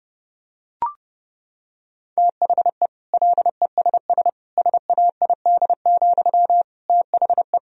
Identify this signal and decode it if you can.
Morse code, heard as a keyed tone, about 30 wpm, 700 Hz. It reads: THE LESS SAID, THE
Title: Sentences from Top 1000 Words - Review Part 2 - 30wpm